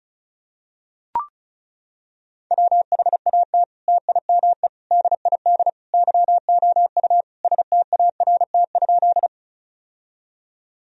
Morse code, heard as a keyed tone, about 36 wpm, 700 Hz. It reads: WHAT TIME DID YOU START?
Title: Sentences from Top 1000 Words - Review Part 1 - 35wpm